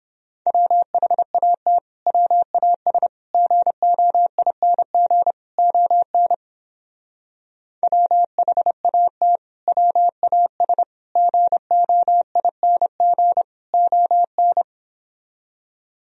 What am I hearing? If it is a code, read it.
Morse code, heard as a keyed tone, about 31 wpm, 700 Hz. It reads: WHAT WAS GOING ON WHAT WAS GOING ON